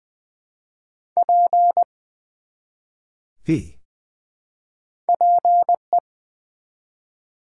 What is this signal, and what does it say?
Morse code, heard as a keyed tone, about 20 wpm, 690 Hz. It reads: P PE
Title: Build A Word - 1 Letter At A Time - 3-Letter Words - 20wpm